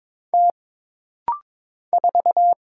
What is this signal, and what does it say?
Morse code, heard as a keyed tone, about 22 wpm, 700 Hz. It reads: T 4
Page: This rapid-fire timing is geared to push your ICR (Instant Character Recognition) skills.